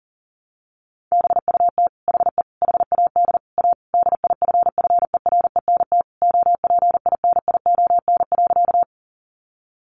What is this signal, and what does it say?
Morse code, heard as a keyed tone, about 40 wpm, 700 Hz. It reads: BUT HE HAD A DIFFERENT OPINION.
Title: Sentences from Top 1200 Words 40wpm